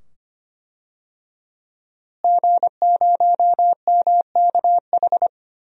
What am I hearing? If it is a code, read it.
Morse code, heard as a keyed tone, about 25 wpm, 700 Hz. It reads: G0MKH
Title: DXCC-100 Call Signs - 5 Characters - Repeated 2x - 25wpm